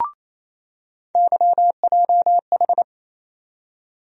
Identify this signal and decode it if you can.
Morse code, heard as a keyed tone, about 28 wpm, 700 Hz. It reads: YJH
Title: USA Call Sign Postfixes - 3 Letters 28wpm